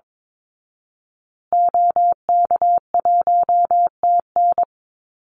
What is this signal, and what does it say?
Morse code, heard as a keyed tone, about 22 wpm, 690 Hz. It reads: OK1TN
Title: Call Signs Repeated 2x - 22wpm